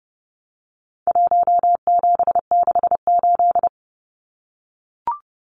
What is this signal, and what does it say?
Morse code, heard as a keyed tone, about 30 wpm, 700 Hz. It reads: 1768
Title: Numbers - Contest Style - 4-Digit 30wpm